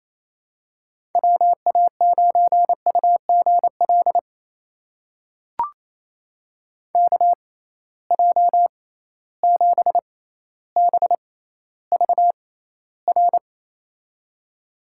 Morse code, heard as a keyed tone, about 28 wpm, spaced as for 14 wpm, 700 Hz. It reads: WA9UGL KJ7BVR
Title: USA Call Signs - 6 Characters - 6x Character Spacing - 28wpm